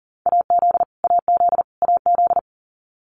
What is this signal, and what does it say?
Morse code, heard as a keyed tone, about 40 wpm, 700 Hz. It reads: AZ AZ AZ
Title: US States Repeated 3x 40wpm